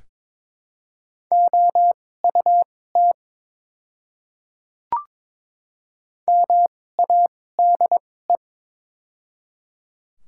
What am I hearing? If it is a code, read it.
Morse code, heard as a keyed tone, about 21 wpm, spaced as for 16 wpm, 700 Hz. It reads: OUT MADE
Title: 2x Character Spacing - Top 200 Words - 22wpm